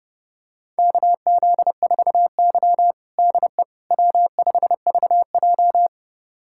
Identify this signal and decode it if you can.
Morse code, heard as a keyed tone, about 30 wpm, 700 Hz. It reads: KZ4Y DE W5VJ
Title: USA Callsign DE Callsign - 4 Characters - 30wpm